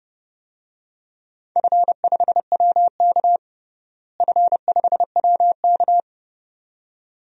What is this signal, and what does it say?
Morse code, heard as a keyed tone, about 30 wpm, 700 Hz. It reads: F5WK F5WK
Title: DXCC-100 Call Signs - 4 Characters - Repeated 2x - 30wpm